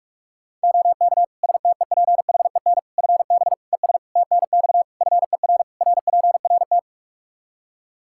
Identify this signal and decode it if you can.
Morse code, heard as a keyed tone, about 41 wpm, 700 Hz. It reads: OK STEPHEN FB ES TNX FER RPRT
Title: Practice QSO - Exchange 2 - Info - 45wpm